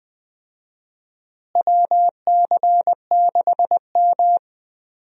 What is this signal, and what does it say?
Morse code, heard as a keyed tone, about 21 wpm, 700 Hz. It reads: WC6M
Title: USA Call Signs - 4 Characters - Repeated 2x - 20wpm